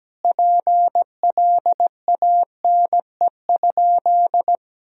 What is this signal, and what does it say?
Morse code, heard as a keyed tone, about 17 wpm, 700 Hz. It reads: PLANE?
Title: Sentences from Top 1000 Words - Review Part 3 - 17wpm